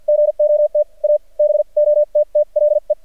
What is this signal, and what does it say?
Morse code, heard as a keyed tone, about 45 wpm, spaced as for 38 wpm, 600 Hz. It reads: GOT A BOTTLE
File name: archived / CRSnet